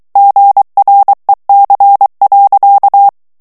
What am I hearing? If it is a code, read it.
Morse code, heard as a keyed tone, about 23 wpm, 800 Hz. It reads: GREC.
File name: CW / FICHIERS-MP3 / FRANCAIS TEXTE CLAIR 2 VITESSE 22 MOTS MINUTES